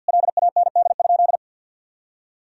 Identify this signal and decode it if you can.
Morse code, heard as a keyed tone, about 53 wpm, 700 Hz. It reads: LAND?
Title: Sentences from Top 1000 Words - Review Part 2 - 50wpm